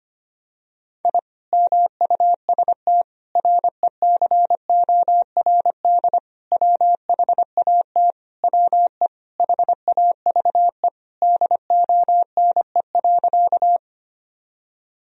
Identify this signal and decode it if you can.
Morse code, heard as a keyed tone, about 25 wpm, 700 Hz. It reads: I MUST RECORD WHAT WE HAVE DONE.
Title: Sentences from Top 1000 Words - Review Part 3 - 25wpm